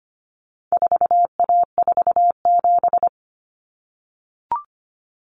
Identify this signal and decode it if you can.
Morse code, heard as a keyed tone, about 25 wpm, 700 Hz. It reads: 4A47
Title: Common Cut Numbers - 4-Digit 25wpm